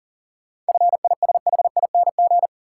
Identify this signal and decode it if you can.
Morse code, heard as a keyed tone, about 40 wpm, 700 Hz. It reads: FISHING
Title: Sentences from Top 1000 Words - Review Part 5 - New Word Drill - 40wpm